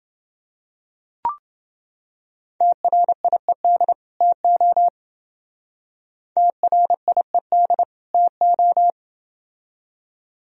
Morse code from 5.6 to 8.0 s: TRIED

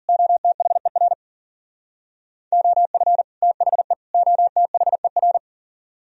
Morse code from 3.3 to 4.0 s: THE

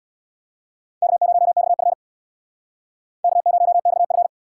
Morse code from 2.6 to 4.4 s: K0XF

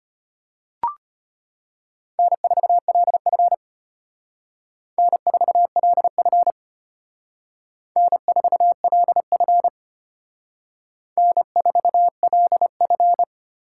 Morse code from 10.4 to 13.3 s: N4LF

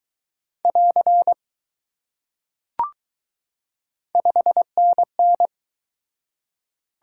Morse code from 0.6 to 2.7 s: <AR>

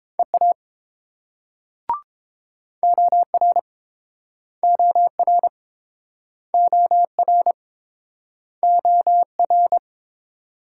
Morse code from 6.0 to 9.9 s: OR OR

